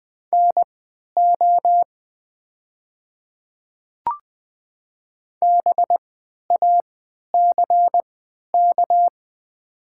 Morse code, heard as a keyed tone, about 20 wpm, spaced as for 11 wpm, 700 Hz. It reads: NO BACK
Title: 3x Character Spacing - Top 300 Words - 20wpm